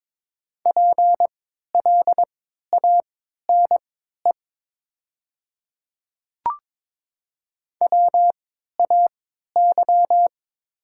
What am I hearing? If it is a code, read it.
Morse code, heard as a keyed tone, about 21 wpm, spaced as for 12 wpm, 700 Hz. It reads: PLANE WAY
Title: 3x Character Spacing - Top 500 Words - 22wpm